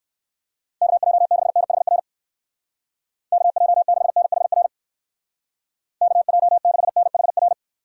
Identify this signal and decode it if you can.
Morse code, heard as a keyed tone, about 57 wpm, 700 Hz. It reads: KJ6NHR KJ6NHR KJ6NHR